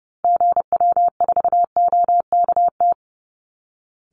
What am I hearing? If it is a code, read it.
Morse code, heard as a keyed tone, about 30 wpm, 700 Hz. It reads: GW4OKT
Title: Call Signs Repeated 2x - 30wpm